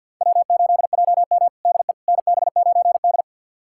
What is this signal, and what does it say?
Morse code, heard as a keyed tone, about 52 wpm, 700 Hz. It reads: W8JM DE NB9D
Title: USA Callsign DE Callsign - 4 Characters - 50wpm